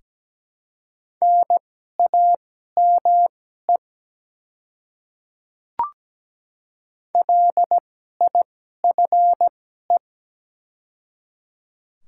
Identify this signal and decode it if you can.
Morse code, heard as a keyed tone, about 17 wpm, spaced as for 12 wpm, 700 Hz. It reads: NAME LIFE
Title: 2x Character Spacing - Top 300 Words - 17wpm